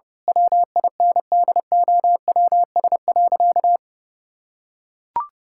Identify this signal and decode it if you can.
Morse code, heard as a keyed tone, about 28 wpm, 700 Hz. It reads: WINDOWS.
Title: Pangram Challenge v2 - 30wpm